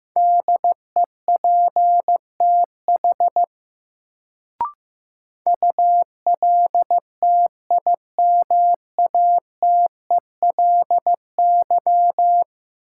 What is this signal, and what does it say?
Morse code, heard as a keyed tone, about 15 wpm, 700 Hz. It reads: DEPTH ULTIMATELY
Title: Sentences from Top 1200 Words - New Word Drill Alt - 15wpm